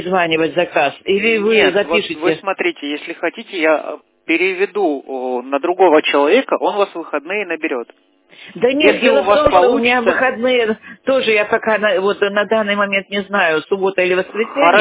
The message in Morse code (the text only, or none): none